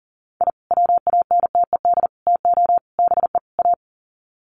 Morse code, heard as a keyed tone, about 40 wpm, 700 Hz. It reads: I WANTED TO BE A
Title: Sets of 5 Words Encore2 40wpm